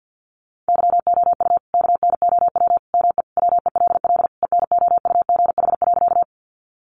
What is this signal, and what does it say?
Morse code, heard as a keyed tone, about 50 wpm, 700 Hz. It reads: YOU KNOW ME WELL ENOUGH.
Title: Sentences from Top 400 Words 50wpm